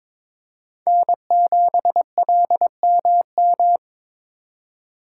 Morse code, heard as a keyed tone, about 26 wpm, 700 Hz. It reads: N7LMM
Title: Speed Racing - USA 1x3 Call Signs 22wpm